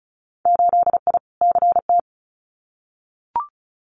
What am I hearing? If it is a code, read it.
Morse code, heard as a keyed tone, about 36 wpm, 700 Hz. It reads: 8I CT
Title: Winter Field Day Exchanges Encore 35wpm